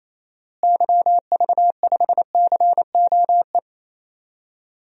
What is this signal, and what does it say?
Morse code, heard as a keyed tone, about 28 wpm, 700 Hz. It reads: YV5COE
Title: DXCC-100 Call Signs - 6 Characters - Repeated 3x - 28wpm